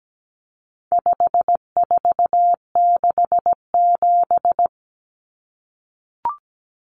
Morse code, heard as a keyed tone, about 17 wpm, 700 Hz. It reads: 5467
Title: Common Cut Numbers - 4-Digit 17wpm